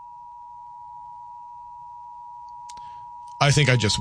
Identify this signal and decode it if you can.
Emergency Alert System (SAME) tones (the 853 and 960 Hz attention tone), with speech. It is heard from 0.2 to 3.4 s.